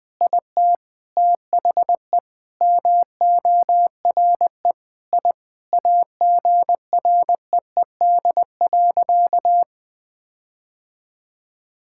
Morse code, heard as a keyed tone, about 21 wpm, 700 Hz. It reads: IT THE MORE I AGREED.
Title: Sentences with 8-Word Phrases - 20wpm